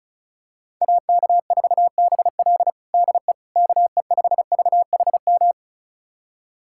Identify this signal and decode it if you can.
Morse code, heard as a keyed tone, about 36 wpm, 700 Hz. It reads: AK4BL DE KE5VHM
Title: USA Callsign DE Callsign 35wpm